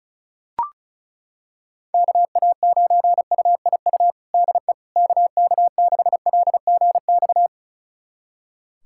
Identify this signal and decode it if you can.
Morse code, heard as a keyed tone, about 36 wpm, 700 Hz. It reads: KA9UIU DE KK6LGX
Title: USA Callsign DE Callsign - 6 Characters - 35wpm